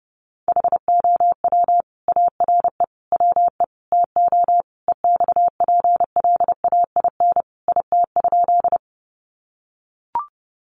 Morse code, heard as a keyed tone, about 29 wpm, 700 Hz. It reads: HOW ARE WE TO EXPLAIN IT?
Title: Sentences from Top 1100 Words 30wpm